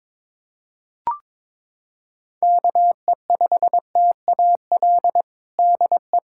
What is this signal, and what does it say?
Morse code, heard as a keyed tone, about 22 wpm, 700 Hz. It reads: KE5TAL DE
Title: USA Callsign DE Callsign - 6 Characters - 22wpm